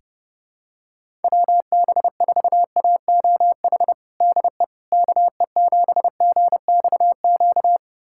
Morse code, heard as a keyed tone, about 30 wpm, 700 Hz. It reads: WB4AOH DE KE7GXQ
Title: USA Callsign DE Callsign 30wpm